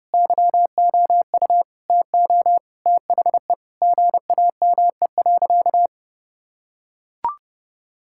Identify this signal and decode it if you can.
Morse code, heard as a keyed tone, about 30 wpm, 700 Hz. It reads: YOU TO THE GAME.
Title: Sentences from Top 1000 Words - Review Part 6 - 30wpm